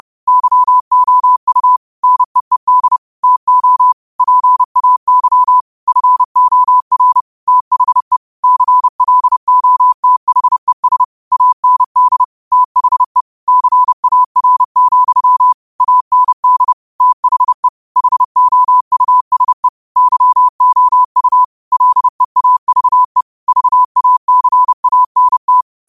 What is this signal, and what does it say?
Morse code, heard as a keyed tone, about 30 wpm, 1000 Hz. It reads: YOU NEED TO PAY FOR THE CLOTHES AND THE CAR, AND THE HOUSE YOU LEAVE VACANT
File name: qotdfiles / 30wpm